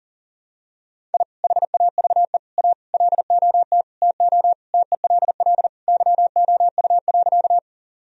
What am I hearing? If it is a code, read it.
Morse code, heard as a keyed tone, about 40 wpm, 700 Hz. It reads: I HAVE A LOT TO TELL YOU.
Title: Sentences from Top 1000 Words - Review Part 3 - 40wpm